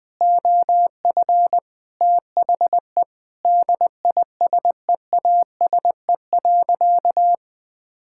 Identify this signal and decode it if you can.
Morse code, heard as a keyed tone, about 20 wpm, 700 Hz. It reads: OF THE DISEASE.
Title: Sentences from Top 1000 Words - Review Part 6 - 20wpm